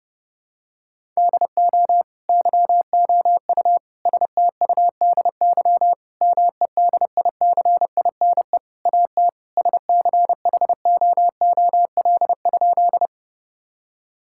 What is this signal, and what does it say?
Morse code, heard as a keyed tone, about 30 wpm, 700 Hz. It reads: DO YOU STUDY MEDICINE AT SCHOOL?
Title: Sentences from Top 1000 Words - Review Part 6 - 30wpm